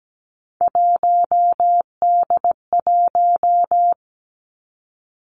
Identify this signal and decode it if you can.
Morse code, heard as a keyed tone, about 17 wpm, 700 Hz. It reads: 1D1
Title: Three Challenging Letter-Number Combinations 17wpm